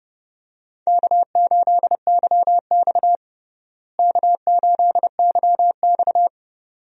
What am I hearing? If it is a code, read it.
Morse code, heard as a keyed tone, about 32 wpm, 700 Hz. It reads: K8YX K8YX
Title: USA Call Signs - 4 Characters - Repeated 2x - 30wpm